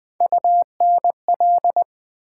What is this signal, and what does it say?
Morse code, heard as a keyed tone, about 21 wpm, 700 Hz. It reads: UNL
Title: Build A Word - 3 Letters At A Time - 6-Letter Words - 20wpm